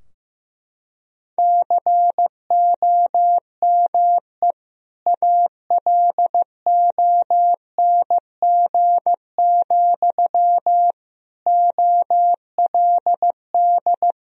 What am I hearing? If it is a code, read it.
Morse code, heard as a keyed tone, about 14 wpm, 700 Hz. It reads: COME ALONG, OLD
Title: Sentences from Top 1000 Words - Review Part 2 - 15wpm